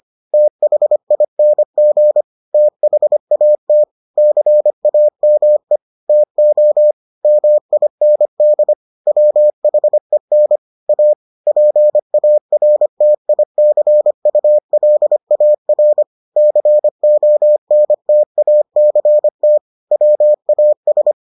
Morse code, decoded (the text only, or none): THING THAT CAME TO MIND WHEN A PARTICULAR CONTACT WAS